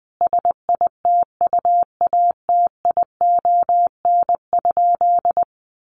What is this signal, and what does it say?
Morse code, heard as a keyed tone, about 20 wpm, 700 Hz. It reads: SITUATION?
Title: Sentences from Top 1200 Words 20wpm